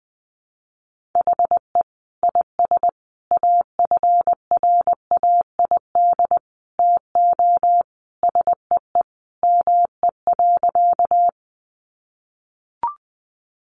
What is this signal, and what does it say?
Morse code, heard as a keyed tone, about 19 wpm, 700 Hz. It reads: HE IS AFRAID TO SEE ME.
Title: Sentences from Top 1500 Words 20wpm